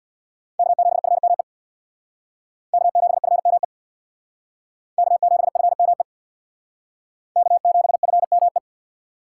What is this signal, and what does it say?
Morse code, heard as a keyed tone, about 61 wpm, 700 Hz. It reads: K7FME K7FME K7FME K7FME
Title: Speed Racing - USA Call Signs 50wpm